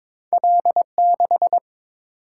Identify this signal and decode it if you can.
Morse code, heard as a keyed tone, about 22 wpm, 690 Hz. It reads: L6
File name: practice / 22wpm / Two Letter-Number Combinations - 22wpm